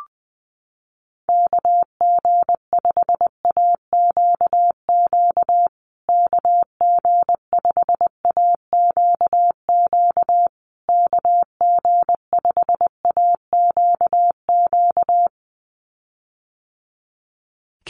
Morse code, heard as a keyed tone, about 20 wpm, 700 Hz. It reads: KG5AQQ KG5AQQ KG5AQQ
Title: USA Call Signs Repeated 3x 20wpm